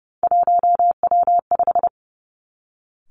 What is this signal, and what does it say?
Morse code, heard as a keyed tone, about 30 wpm, 700 Hz. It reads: 1W5
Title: Three Challenging Letter-Number Combinations 30wpm